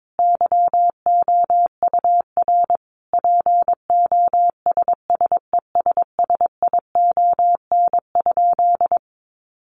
Morse code, heard as a keyed tone, about 22 wpm, 700 Hz. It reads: YOUR POSSESSION?
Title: Sentences from Top 1400 Words 22wpm